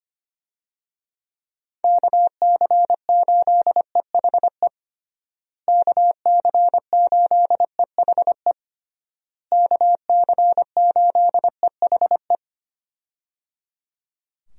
Morse code, heard as a keyed tone, about 25 wpm, 700 Hz. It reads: KC8EHE KC8EHE KC8EHE
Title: USA Call Signs - 6 Characters - Repeated 3x - 25wpm